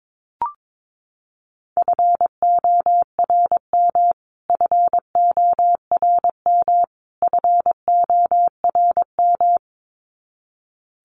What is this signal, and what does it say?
Morse code, heard as a keyed tone, about 22 wpm, 700 Hz. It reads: FORM FORM FORM
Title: Top 500 Words Repeated 22wpm